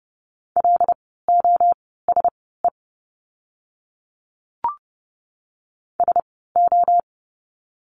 Morse code, heard as a keyed tone, about 30 wpm, spaced as for 17 wpm, 700 Hz. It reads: LOSE SO
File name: Lesson 17f - Letter u - Words 30wpm